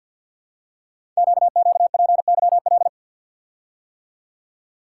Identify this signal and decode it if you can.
Morse code, heard as a keyed tone, about 50 wpm, 700 Hz. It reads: QQPYL
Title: Code Groups - 5 Letters - 50wpm